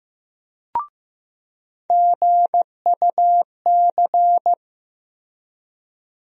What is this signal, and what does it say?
Morse code, heard as a keyed tone, about 15 wpm, 700 Hz. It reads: GUC
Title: USA Call Sign Postfixes - 3 Letters 15wpm